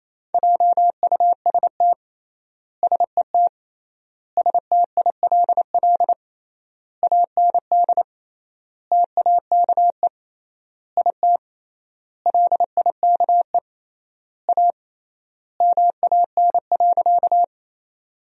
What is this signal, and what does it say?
Morse code, heard as a keyed tone, about 28 wpm, 700 Hz. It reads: JUST SET STILL AND TAKE IT LIKE A MAN.
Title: 3x Word-Spacing - Sentences from Top 300 Words - 28wpm